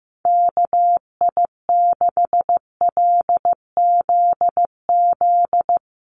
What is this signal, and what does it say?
Morse code, heard as a keyed tone, about 15 wpm, 700 Hz. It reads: KI6LZZ
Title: USA Call Signs Repeated 3x 15wpm